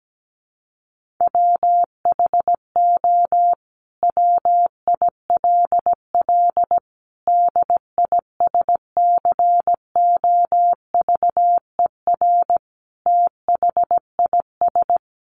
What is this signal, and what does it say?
Morse code, heard as a keyed tone, about 17 wpm, 700 Hz. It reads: WHO WILL DISCOVER THIS
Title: Sentences from Top 1100 Words 17wpm